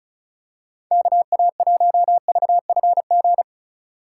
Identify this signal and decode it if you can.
Morse code, heard as a keyed tone, about 34 wpm, 700 Hz. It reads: KA1VFG
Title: USA Call Signs - 6 Characters - 2x Character Spacing - 35wpm